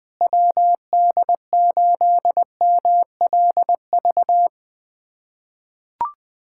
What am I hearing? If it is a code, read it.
Morse code, heard as a keyed tone, about 21 wpm, 700 Hz. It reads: WD8MLV
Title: USA Callsign DE Callsign - 6 Characters - 20wpm